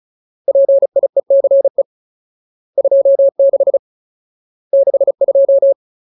Morse code, heard as a keyed tone, about 34 wpm, 550 Hz. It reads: PIECE 26 62
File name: Ultimate Word List / 35WPM Ultimate